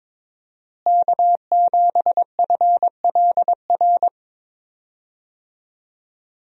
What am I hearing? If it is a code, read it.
Morse code, heard as a keyed tone, about 21 wpm, 700 Hz. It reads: K7FLR
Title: USA Call Signs - 5 Characters - Repeated 3x - 22wpm